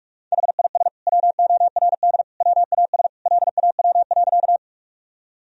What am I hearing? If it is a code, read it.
Morse code, heard as a keyed tone, about 41 wpm, 700 Hz. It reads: HIS WORD WAS LAW.
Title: Sentences from Top 1000 Words - Review Part 3 - 45wpm